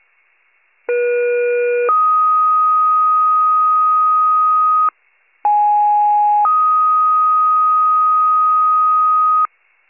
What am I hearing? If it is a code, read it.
Two-tone page, heard as a one-second tone, then a three-second tone, about 480 then 1250 Hz, about 830 then 1250 Hz. It is heard from 0.9 to 9.5 s.